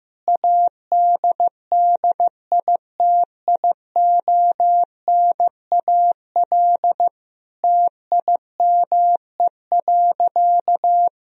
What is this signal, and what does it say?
Morse code, heard as a keyed tone, about 15 wpm, 700 Hz. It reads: ADDITIONAL TIME.
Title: Sentences from Top 1000 Words - Review Part 4 - 15wpm